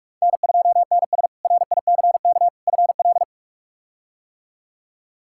Morse code, heard as a keyed tone, about 46 wpm, 700 Hz. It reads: N2NS RICK FL
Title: K1USN SST QSO Practice - 45wpm